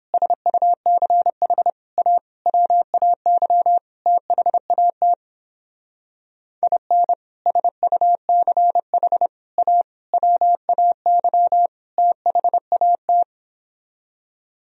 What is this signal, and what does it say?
Morse code, heard as a keyed tone, about 29 wpm, 700 Hz. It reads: SUCH A WAY THAT IN SUCH A WAY THAT